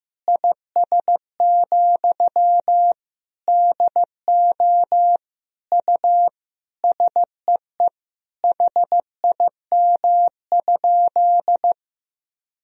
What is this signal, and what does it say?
Morse code, heard as a keyed tone, about 15 wpm, 700 Hz. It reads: IS, DO U SEE HIM?
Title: Sentences from Top 100 Words and Abbreviations 15wpm